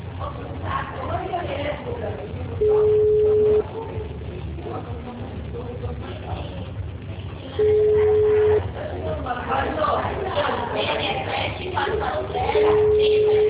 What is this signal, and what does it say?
Ringback tone (Europe, 425 Hz), with speech.
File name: out- -2008- - - .4901